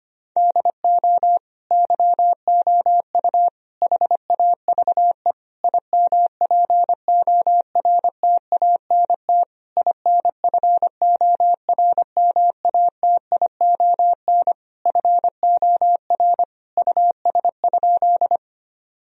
Morse code, heard as a keyed tone, about 25 wpm, 700 Hz. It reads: DO YOU HAVE IMPORTANT INFORMATION FOR US?
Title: Sentences from Top 1000 Words - Review Part 3 - 25wpm